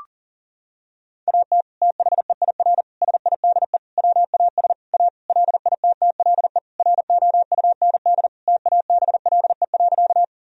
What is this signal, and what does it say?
Morse code, heard as a keyed tone, about 40 wpm, 700 Hz. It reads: AT THEIR SIDE WAS A LITTLE ROUND TABLE.
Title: Sentences from Top 1000 Words - Review Part 2 - 40wpm